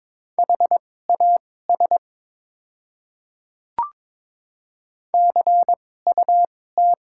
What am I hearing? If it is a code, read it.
Morse code, heard as a keyed tone, about 22 wpm, spaced as for 16 wpm, 700 Hz. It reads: HAS CUT
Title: 2x Character Spacing - Top 300 Words - 22wpm